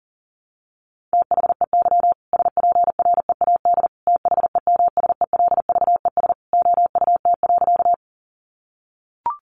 Morse code, heard as a keyed tone, about 40 wpm, 700 Hz. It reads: THEY SPREAD THEMSELVES OUT.
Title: Sentences from Top 1200 Words 40wpm